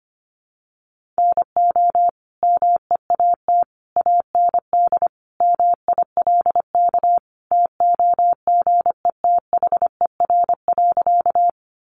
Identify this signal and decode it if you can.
Morse code, heard as a keyed tone, about 25 wpm, 700 Hz. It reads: NO MEAT AND MILK TOGETHER.
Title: Sentences from Top 1500 Words 25wpm